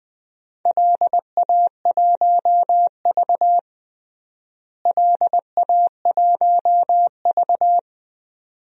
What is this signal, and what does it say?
Morse code, heard as a keyed tone, about 20 wpm, 700 Hz. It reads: LA1V LA1V
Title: DXCC-100 Call Signs - 4 Characters - Repeated 2x - 20wpm